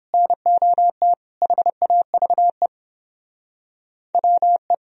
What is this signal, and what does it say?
Morse code, heard as a keyed tone, about 31 wpm, 700 Hz. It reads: NOT HAVE WE